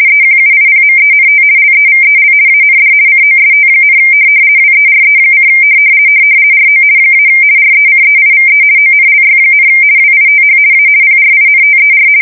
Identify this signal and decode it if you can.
RTTY, about 50 baud. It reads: EQUIPMENT LEVEL AND COST, WERE: RO - RECEIVE ONLY, WITH A PRINTER
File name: fsk170-50B / teletype-corp